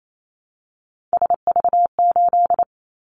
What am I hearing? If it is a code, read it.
Morse code, heard as a keyed tone, about 28 wpm, 700 Hz. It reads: SV8
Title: Callsign Prefixes for Top 50 DXCC - 28wpm